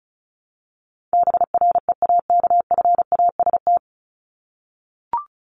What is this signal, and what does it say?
Morse code, heard as a keyed tone, about 34 wpm, 700 Hz. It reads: BREAKFAST
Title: Top 1251-1500 Words 35wpm